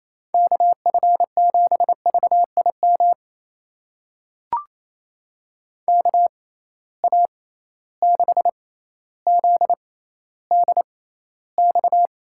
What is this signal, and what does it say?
Morse code, heard as a keyed tone, about 28 wpm, spaced as for 14 wpm, 700 Hz. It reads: KF7VIM KA6ZDX
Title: USA Call Signs - 6 Characters - 6x Character Spacing - 28wpm